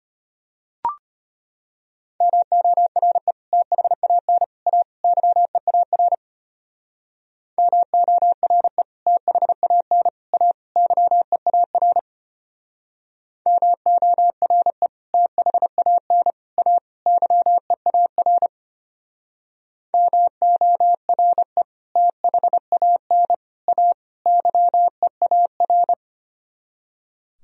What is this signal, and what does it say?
Morse code, heard as a keyed tone, about 32 wpm, 700 Hz. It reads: MORE THAN A YEAR MORE THAN A YEAR MORE THAN A YEAR MORE THAN A YEAR